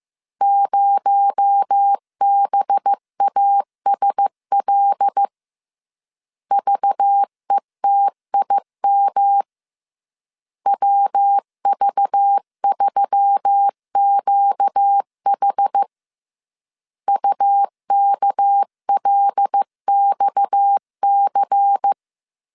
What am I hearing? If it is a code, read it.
Morse code, heard as a keyed tone, about 15 wpm, 800 Hz. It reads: 0BASL VETIM WV3QH UKLXC
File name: CW / DET / 1 6 1500C S B 14wpm 2958